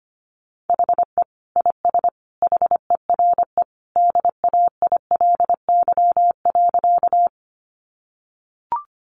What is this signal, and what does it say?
Morse code, heard as a keyed tone, about 25 wpm, 700 Hz. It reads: HE IS HERE DAILY.
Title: Sentences from Top 1100 Words 25wpm